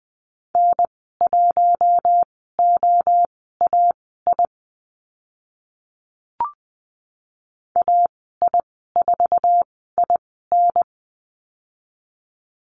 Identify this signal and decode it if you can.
Morse code, heard as a keyed tone, about 20 wpm, spaced as for 14 wpm, 700 Hz. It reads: N1OAI AI4IN
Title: 048 Lesson 9g - Number 4 - Call Signs 20wpm